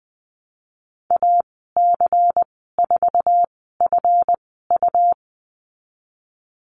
Morse code, heard as a keyed tone, about 20 wpm, spaced as for 15 wpm, 700 Hz. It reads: AC4FU
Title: 193 Lesson 31g - Letter v - Call Signs 20wpm